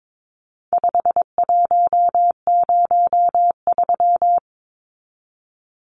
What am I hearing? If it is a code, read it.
Morse code, heard as a keyed tone, about 22 wpm, 700 Hz. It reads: 5103
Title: Numbers - 4-Digit 22wpm